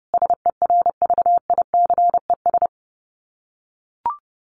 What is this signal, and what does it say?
Morse code, heard as a keyed tone, about 28 wpm, 700 Hz. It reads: SERVICES
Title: Sets of 5 Words 30wpm